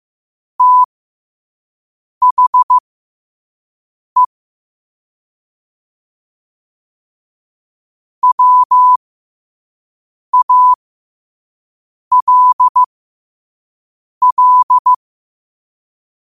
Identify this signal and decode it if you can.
Morse code, heard as a keyed tone, about 15 wpm, spaced as for 5 wpm, 1000 Hz. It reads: THE WALL
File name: qotdfiles / 5wpm